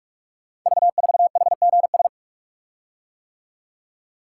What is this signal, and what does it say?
Morse code, heard as a keyed tone, about 44 wpm, 700 Hz. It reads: V4HGS
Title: Code Groups - 5 Letter:Numbers - 45wpm